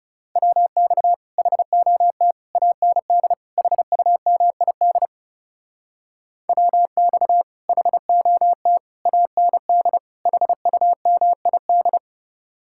Morse code, heard as a keyed tone, about 40 wpm, 700 Hz. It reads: WX HOT AND HUMID WX HOT AND HUMID